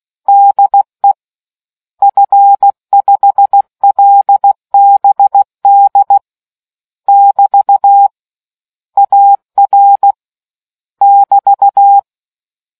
Morse code, heard as a keyed tone, about 16 wpm, 800 Hz. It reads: DE F5LBD <BT> AR <BT>